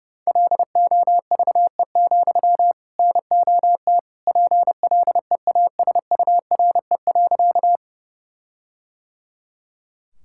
Morse code, heard as a keyed tone, about 32 wpm, 700 Hz. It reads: LOVE, NOT PLEASURE.
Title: Sentences from Top 1600 Words 30wpm